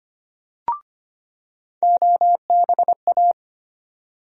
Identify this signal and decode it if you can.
Morse code, heard as a keyed tone, about 25 wpm, 700 Hz. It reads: OBA
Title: USA Call Sign Postfixes - 3 Letters 25wpm